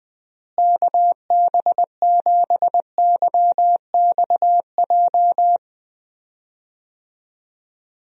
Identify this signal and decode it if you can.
Morse code, heard as a keyed tone, about 21 wpm, 700 Hz. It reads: KB7YXJ
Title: USA Call Signs - 6 Characters - Repeated 2x - 20wpm